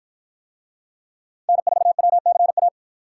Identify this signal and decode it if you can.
Morse code, heard as a keyed tone, about 53 wpm, 700 Hz. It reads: N4WCA